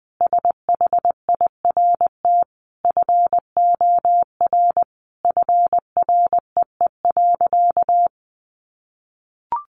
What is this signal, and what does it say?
Morse code, heard as a keyed tone, about 20 wpm, 700 Hz. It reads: SHIRT FOR FREE.
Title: Additional Easy Phrases 20wpm